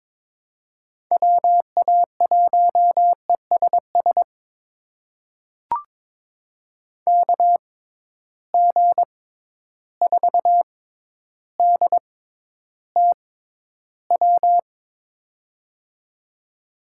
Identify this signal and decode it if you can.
Morse code, heard as a keyed tone, about 22 wpm, spaced as for 11 wpm, 700 Hz. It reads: WA1ESS KG4DTW
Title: USA Call Signs - 6 Characters - 6x Character Spacing - 22wpm